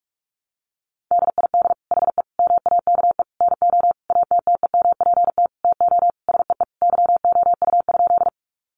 Morse code, heard as a keyed tone, about 44 wpm, 700 Hz. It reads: DID HE MAKE NO ATTEMPT TO SEE YOU?
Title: Sentences from Top 1000 Words 45wpm